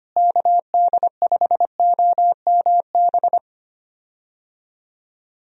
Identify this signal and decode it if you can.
Morse code, heard as a keyed tone, about 25 wpm, 700 Hz. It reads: KD5OMB
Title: USA Call Signs - 6 Characters - Repeated 3x - 25wpm